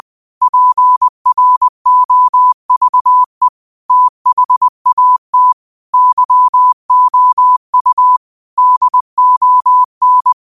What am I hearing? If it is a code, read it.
Morse code, heard as a keyed tone, about 20 wpm, 1000 Hz. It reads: PROVE THAT YOU DON
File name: qotdfiles / 20wpm